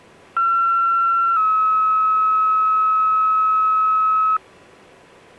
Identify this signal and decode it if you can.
Two-tone page, heard as a one-second tone, then a three-second tone, about 1360 then 1290 Hz. It is heard from 0.4 to 4.4 s.